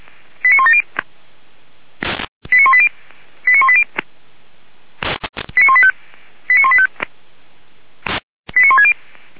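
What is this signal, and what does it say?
Five-tone selective call (ZVEI), heard as quick runs of five tones, tones of about 70 ms. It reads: 89170 89180 89180 89185 89185 89160
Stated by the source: ZVEI five-tone call 89138 received by SDR, logged by Feuerwehr Orlamünde.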